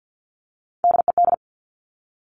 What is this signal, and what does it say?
Morse code, heard as a keyed tone, about 50 wpm, 700 Hz. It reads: DED
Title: Top Three Letter Combinations in Words 50wpm